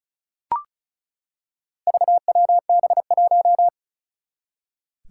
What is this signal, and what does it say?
Morse code, heard as a keyed tone, about 36 wpm, 700 Hz. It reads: VWB1
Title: Four Challenging Letter-Number Combinations 35wpm